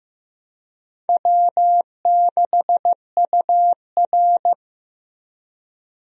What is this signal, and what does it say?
Morse code, heard as a keyed tone, about 15 wpm, 700 Hz. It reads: W6UR
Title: USA Call Signs - 4 Characters - Repeated 3x - 15wpm